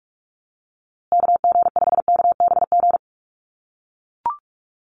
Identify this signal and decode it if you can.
Morse code, heard as a keyed tone, about 44 wpm, 700 Hz. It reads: KG5KBG
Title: USA Call Signs Repeated 3x 45wpm